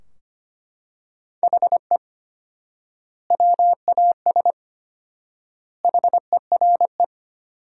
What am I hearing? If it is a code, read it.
Morse code, heard as a keyed tone, about 25 wpm, 700 Hz. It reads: HE WAS HERE
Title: 4x Word-Spacing - Sentences from Top 500 Words - 25wpm